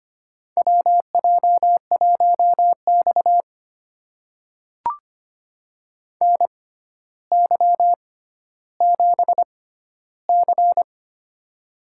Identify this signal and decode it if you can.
Morse code, heard as a keyed tone, about 25 wpm, spaced as for 13 wpm, 700 Hz. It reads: WJ1X NY7C
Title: USA Call Signs - 4 Characters - 6x Character Spacing - 25wpm